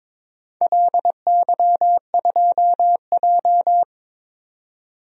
Morse code, heard as a keyed tone, about 22 wpm, 700 Hz. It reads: LY2J
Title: DXCC-100 Call Signs - 4 Characters - 22wpm